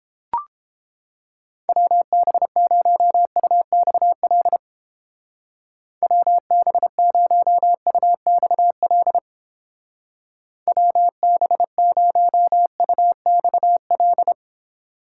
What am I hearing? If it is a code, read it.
Morse code, heard as a keyed tone, about 31 wpm, 700 Hz. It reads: WB0UXL WB0UXL WB0UXL